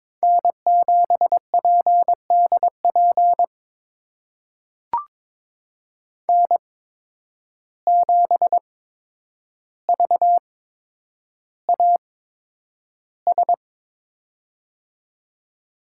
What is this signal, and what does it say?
Morse code, heard as a keyed tone, about 22 wpm, spaced as for 9 wpm, 700 Hz. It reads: N7PDP N7VAS
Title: USA Call Signs - 5 Characters - 8x Character Spacing - 22wpm